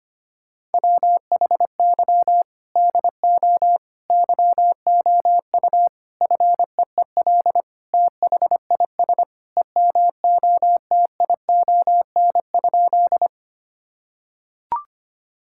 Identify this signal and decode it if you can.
Morse code, heard as a keyed tone, about 25 wpm, 700 Hz. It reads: WHY DO YOU FEEL THIS EMOTION?
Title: Sentences from Top 1600 Words 25wpm